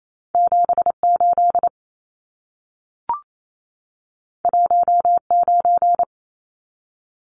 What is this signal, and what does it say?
Morse code, heard as a keyed tone, about 28 wpm, 700 Hz. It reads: 78 19
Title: Numbers - 2-Digit 28wpm